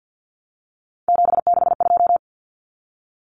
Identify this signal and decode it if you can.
Morse code, heard as a keyed tone, about 50 wpm, 700 Hz. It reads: 762
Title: Numbers - 3-Digit 50wpm